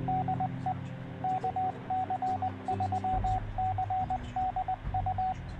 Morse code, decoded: DE KC3CDU